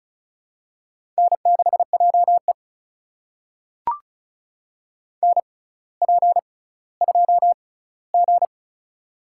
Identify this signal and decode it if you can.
Morse code, heard as a keyed tone, about 36 wpm, spaced as for 18 wpm, 700 Hz. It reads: N6JE NP2G
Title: USA Call Signs - 4 Characters - 6x Character Spacing - 35wpm